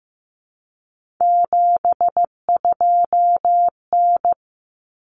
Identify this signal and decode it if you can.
Morse code, heard as a keyed tone, about 15 wpm, 700 Hz. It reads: 72N
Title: Common Cut Numbers - 3-Digit 15wpm